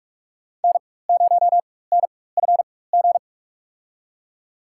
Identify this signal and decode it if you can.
Morse code, heard as a keyed tone, about 47 wpm, spaced as for 21 wpm, 700 Hz. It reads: N0NFG
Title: USA Call Signs - 5 Characters - 4x Character Spacing - 45wpm